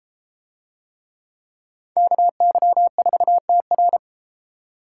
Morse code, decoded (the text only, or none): KY4TR